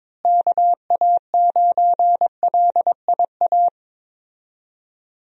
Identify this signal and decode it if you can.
Morse code, heard as a keyed tone, about 22 wpm, 700 Hz. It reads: KA9LIA
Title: USA Call Signs - 6 Characters - Repeated 2x - 22wpm